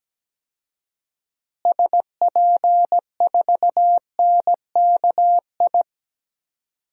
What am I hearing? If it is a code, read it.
Morse code, heard as a keyed tone, about 17 wpm, 700 Hz. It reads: SP4NKI
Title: DXCC-100 Call Signs - 6 Characters - Repeated 2x - 17wpm